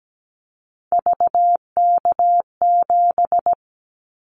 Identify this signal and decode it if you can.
Morse code, heard as a keyed tone, about 17 wpm, 700 Hz. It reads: VK7
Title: Callsign Prefixes for Top 100 DXCC - 17wpm